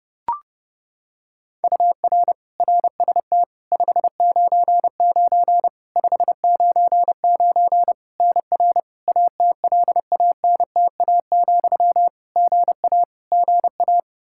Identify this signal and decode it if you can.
Morse code, heard as a keyed tone, about 30 wpm, 700 Hz. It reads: UR RST 599 599 NR ATLANTA, GA GA
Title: Practice QSO - Exchange 1 - Beginning - 30wpm